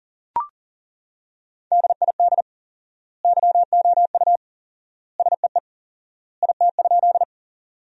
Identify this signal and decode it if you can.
Morse code, heard as a keyed tone, about 40 wpm, 700 Hz. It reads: DID YOU SEE IT?
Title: 4x Word-Spacing - Sentences from Top 100 Words - 40wpm